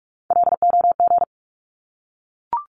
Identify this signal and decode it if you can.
Morse code, heard as a keyed tone, about 44 wpm, 700 Hz. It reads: LOG
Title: Top 1251-1500 Words 45wpm